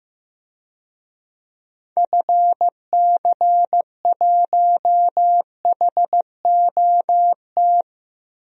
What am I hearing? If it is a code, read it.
Morse code, heard as a keyed tone, about 15 wpm, 700 Hz. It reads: FC1HOT
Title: DXCC-100 Call Signs - 6 Characters - 15wpm